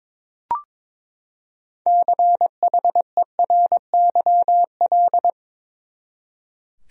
Morse code, heard as a keyed tone, about 22 wpm, 700 Hz. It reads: CHERYL
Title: USA Common Names - 22wpm